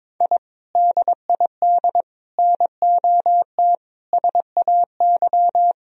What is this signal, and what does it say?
Morse code, heard as a keyed tone, about 22 wpm, 700 Hz. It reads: I DID NOT SAY
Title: Sentences from Top 1000 Words - Review Part 3 - 22wpm